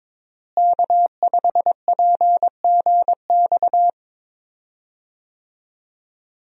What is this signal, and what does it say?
Morse code, heard as a keyed tone, about 22 wpm, 700 Hz. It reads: K5PGX
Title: USA Call Signs - 5 Characters - Repeated 2x - 22wpm